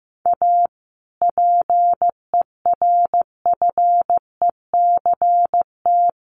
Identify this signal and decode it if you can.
Morse code, heard as a keyed tone, about 15 wpm, 700 Hz. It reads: A PERFECT
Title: Sentences from Top 1000 Words 15wpm